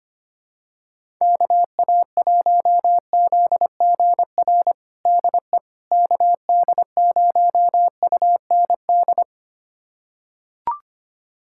Morse code, heard as a keyed tone, about 25 wpm, 700 Hz. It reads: KA1ZGR DE KD0UND
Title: USA Callsign DE Callsign - 6 Characters - 25wpm